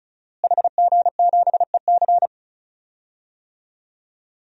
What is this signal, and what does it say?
Morse code, heard as a keyed tone, about 36 wpm, 700 Hz. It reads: HG7EC
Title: DXCC-100 Call Signs - 5 Characters - Repeated 3x - 35wpm